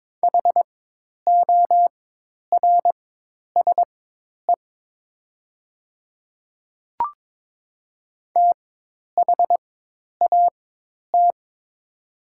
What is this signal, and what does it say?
Morse code, heard as a keyed tone, about 22 wpm, spaced as for 10 wpm, 700 Hz. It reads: HORSE THAT
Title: 4x Character Spacing - Top 300 Words - 22wpm